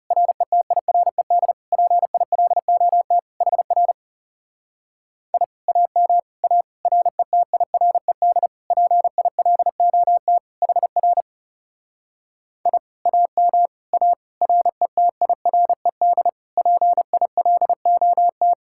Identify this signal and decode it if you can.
Morse code, heard as a keyed tone, about 40 wpm, 700 Hz. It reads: RETIRED PILOT HR I AM A RETIRED PILOT HR I AM A RETIRED PILOT